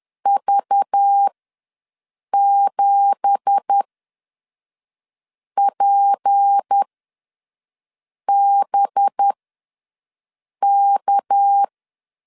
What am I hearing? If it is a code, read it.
Morse code, heard as a keyed tone, about 10 wpm, spaced as for 5 wpm, 800 Hz. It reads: V7PBK
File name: LEZIONI / 9 2867 ws 10wpm 4215